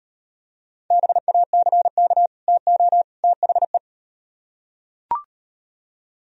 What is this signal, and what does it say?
Morse code, heard as a keyed tone, about 38 wpm, 700 Hz. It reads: BACK TO THE